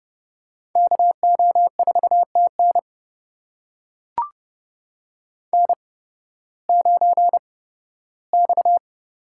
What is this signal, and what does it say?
Morse code, heard as a keyed tone, about 30 wpm, spaced as for 16 wpm, 700 Hz. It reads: KO4TN N9X
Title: USA Call Signs - 5 Characters - 8x Character Spacing - 30wpm